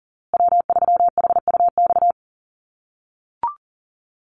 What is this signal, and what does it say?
Morse code, heard as a keyed tone, about 40 wpm, 700 Hz. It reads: W3HUX
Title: USA Call Signs Repeated 2x 40wpm